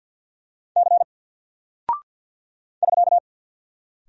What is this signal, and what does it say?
Morse code, heard as a keyed tone, about 48 wpm, 700 Hz. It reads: C <SK>
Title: Single Character (Letter:Number:Punctuation:Prosign) 50wpm